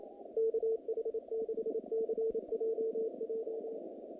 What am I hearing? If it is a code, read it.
Morse code, heard as a keyed tone, about 30 wpm, 470 Hz. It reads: KH6CJJ
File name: contests / naqp-cw-k5tr-op-k5pi- -2300